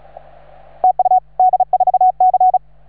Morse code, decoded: TU D4C